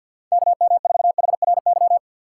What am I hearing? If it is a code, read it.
Morse code, heard as a keyed tone, about 49 wpm, 700 Hz. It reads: KM4HRY
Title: USA Call Signs - 4x Character Spacing - 50wpm